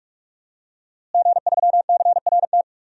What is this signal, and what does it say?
Morse code, heard as a keyed tone, about 44 wpm, 700 Hz. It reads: G3CRT
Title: DXCC-100 Call Signs - 5 Characters - Repeated 2x - 45wpm